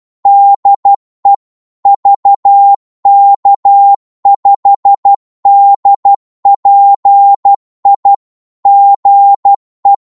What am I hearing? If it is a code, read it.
Morse code, heard as a keyed tone, about 12 wpm, 800 Hz. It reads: DE VK5DPI GE
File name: Morse / qso-12wpm